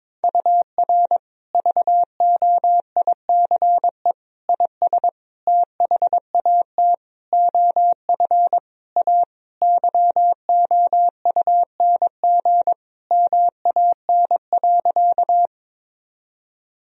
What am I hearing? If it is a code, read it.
Morse code, heard as a keyed tone, about 21 wpm, 700 Hz. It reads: UR VOICE IS THAT OF A YOUNG MAN.
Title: Sentences from Top 500 Words and Abbreviations 22wpm